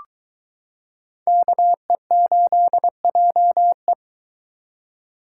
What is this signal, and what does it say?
Morse code, heard as a keyed tone, about 22 wpm, 700 Hz. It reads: KE8JE